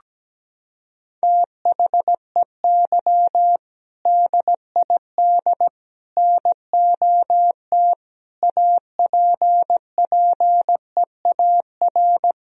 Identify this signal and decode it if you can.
Morse code, heard as a keyed tone, about 17 wpm, 700 Hz. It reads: THEY DID NOT APPEAR
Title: Sentences from Top 1000 Words - Review Part 2 - 17wpm